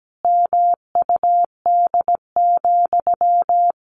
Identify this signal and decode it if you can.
Morse code, heard as a keyed tone, about 17 wpm, 700 Hz. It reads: MUD,
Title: A Sound Of Thunder 17wpm